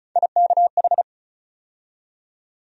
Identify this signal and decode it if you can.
Morse code, heard as a keyed tone, about 36 wpm, 700 Hz. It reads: IKH
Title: USA Call Sign Postfixes - 3 Letters 35wpm